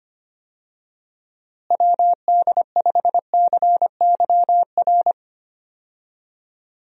Morse code, heard as a keyed tone, about 25 wpm, 700 Hz. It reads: WD5CYR
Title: USA Call Signs - 6 Characters - Repeated 3x - 25wpm